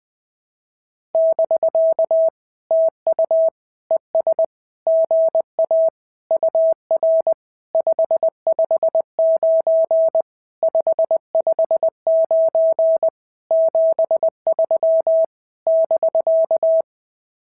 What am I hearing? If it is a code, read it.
Morse code, heard as a keyed tone, about 19 wpm, 650 Hz. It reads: <BK> TU ES GA UR 559 559 73 <BK>
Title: SOTA QSO Practice 20wpm